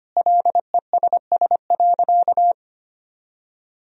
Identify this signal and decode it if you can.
Morse code, heard as a keyed tone, about 25 wpm, 700 Hz. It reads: LESS.
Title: Sentences from Top 1000 Words - Review Part 3 - 25wpm